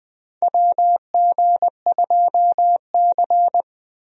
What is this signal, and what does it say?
Morse code, heard as a keyed tone, about 20 wpm, 700 Hz. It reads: WG2C
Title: USA Call Signs - 4 Characters - 20wpm